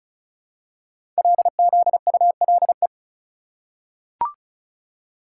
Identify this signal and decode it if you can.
Morse code, heard as a keyed tone, about 36 wpm, 700 Hz. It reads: LZULE
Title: Code Groups - 5 Letters - 35wpm